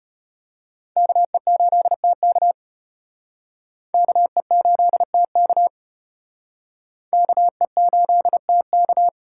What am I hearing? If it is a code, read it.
Morse code, heard as a keyed tone, about 34 wpm, 700 Hz. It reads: KE8TK KE8TK KE8TK